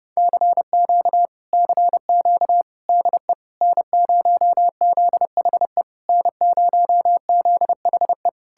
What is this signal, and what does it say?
Morse code, heard as a keyed tone, about 33 wpm, 700 Hz. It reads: CQ CQ DE N0ZHE N0ZHE